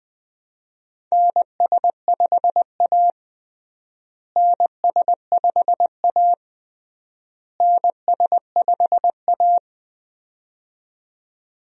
Morse code, heard as a keyed tone, about 20 wpm, 700 Hz. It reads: NS5A NS5A NS5A
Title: USA Call Signs - 4 Characters - Repeated 3x - 20wpm